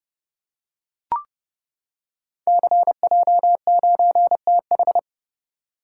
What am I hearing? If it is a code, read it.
Morse code, heard as a keyed tone, about 30 wpm, 700 Hz. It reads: CJ9TH
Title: Code Groups - 5 Letter:Numbers - 30wpm